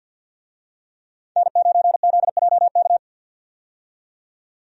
Morse code, heard as a keyed tone, about 51 wpm, 700 Hz. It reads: N9ZJK
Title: USA Call Signs - 5 Characters - 50wpm